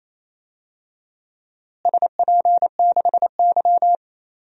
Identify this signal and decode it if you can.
Morse code, heard as a keyed tone, about 27 wpm, 700 Hz. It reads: SP6Y
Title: DXCC-100 Call Signs - 4 Characters - Repeated 3x - 28wpm